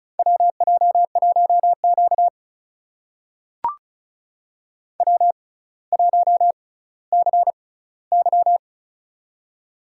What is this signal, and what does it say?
Morse code, heard as a keyed tone, about 36 wpm, spaced as for 18 wpm, 700 Hz. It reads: WJ1Q W1CY
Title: USA Call Signs - 4 Characters - 6x Character Spacing - 35wpm